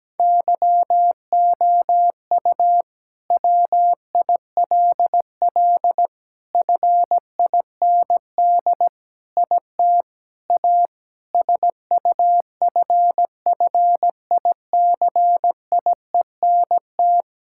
Morse code, heard as a keyed tone, about 17 wpm, 700 Hz. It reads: YOU WILL FIND IT A SUFFICIENT
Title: Sentences from Top 1600 Words 17wpm